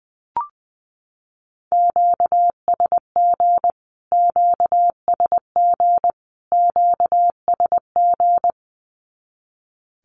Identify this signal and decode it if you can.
Morse code, heard as a keyed tone, about 20 wpm, 700 Hz. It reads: QSG QSG QSG
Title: Q-Codes Repeated 3x 20wpm